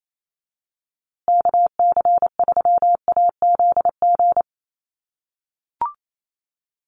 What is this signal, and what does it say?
Morse code, heard as a keyed tone, about 28 wpm, 700 Hz. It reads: KC3AZG
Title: USA Call Signs Repeated 2x 28wpm